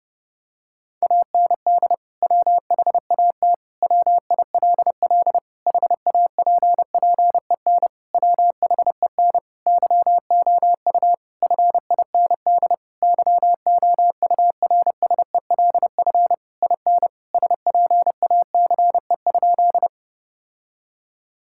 Morse code, heard as a keyed tone, about 30 wpm, 700 Hz. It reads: AND WHAT WILL HAPPEN WHEN YOU FIND YOURSELF IN SPACE?
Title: Sentences from Top 1000 Words - Review Part 5 - 30wpm